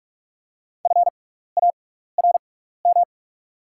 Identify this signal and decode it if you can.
Morse code, heard as a keyed tone, about 48 wpm, spaced as for 16 wpm, 690 Hz. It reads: FARM
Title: 6x Character Spacing - Top 500 Words - 45wpm